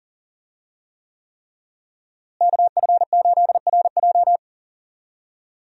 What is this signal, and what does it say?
Morse code, heard as a keyed tone, about 37 wpm, 700 Hz. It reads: KF8RJ
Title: USA Call Signs - 5 Characters - Repeated 3x - 40wpm